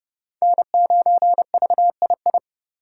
Morse code, heard as a keyed tone, about 29 wpm, 700 Hz. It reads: N9VII
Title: USA Call Signs - 4x Character Spacing - 30wpm